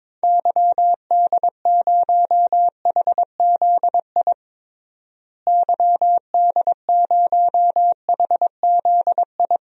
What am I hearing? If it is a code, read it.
Morse code, heard as a keyed tone, about 22 wpm, 700 Hz. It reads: YD0HZI YD0HZI
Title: DXCC-100 Call Signs - 6 Characters - Repeated 3x - 22wpm